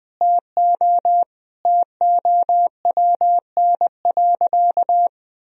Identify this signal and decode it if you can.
Morse code, heard as a keyed tone, about 20 wpm, 700 Hz. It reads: TO TOWN.
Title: Sentences from Top 1000 Words - Review Part 1 - 20wpm